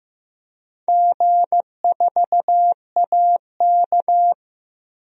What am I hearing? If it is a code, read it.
Morse code, heard as a keyed tone, about 15 wpm, 700 Hz. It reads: G4AK
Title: DXCC-100 Call Signs - 4 Characters - Repeated 2x - 15wpm